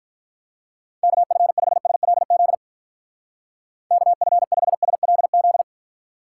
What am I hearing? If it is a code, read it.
Morse code, heard as a keyed tone, about 42 wpm, 700 Hz. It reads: KF5SLZ KF5SLZ
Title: Speed Racing - USA 2x3 Call Signs 35wpm